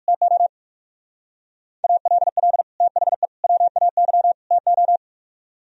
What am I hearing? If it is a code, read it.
Morse code, heard as a keyed tone, about 45 wpm, 700 Hz. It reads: TO ALL THE WAY TO
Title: Speed-Racing - Sets of 4 Words 45wpm